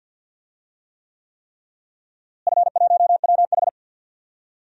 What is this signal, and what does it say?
Morse code, heard as a keyed tone, about 53 wpm, 700 Hz. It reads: F1WH
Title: DXCC-100 Call Signs - 4 Characters - Repeated 3x - 50wpm